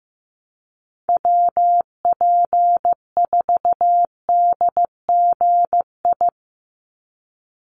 Morse code, heard as a keyed tone, about 15 wpm, 700 Hz. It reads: WP4DGI
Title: USA Call Signs 15wpm